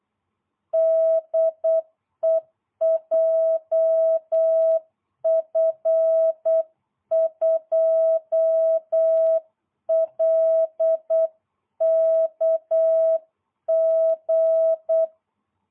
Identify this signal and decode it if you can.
Morse code, heard as a keyed tone, about 8 wpm, 660 Hz. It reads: DE JF2LKG